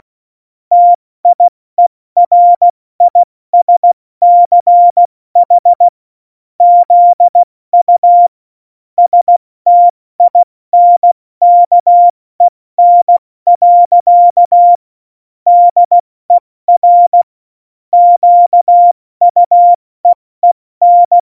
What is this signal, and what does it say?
Morse code, heard as a keyed tone, about 16 wpm, 700 Hz. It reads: TIERISCH ZU STINKEN. DER QUEEN
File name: Morsekurs / Witze / Witz 19 Mai, 2020 70 F80